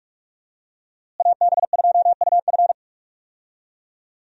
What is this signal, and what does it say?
Morse code, heard as a keyed tone, about 42 wpm, 700 Hz. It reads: AB2UF
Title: USA Call Signs - 5 Characters - Repeated 2x - 45wpm